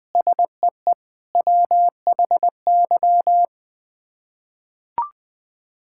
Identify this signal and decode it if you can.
Morse code, heard as a keyed tone, about 20 wpm, 700 Hz. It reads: SEE WHY
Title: Sets of 6 Words - 20wpm